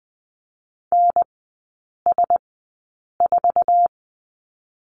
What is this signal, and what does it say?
Morse code, heard as a keyed tone, about 20 wpm, spaced as for 8 wpm, 700 Hz. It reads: NS4
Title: 100 Lesson 17e - Letter u - Three Characters 20wpm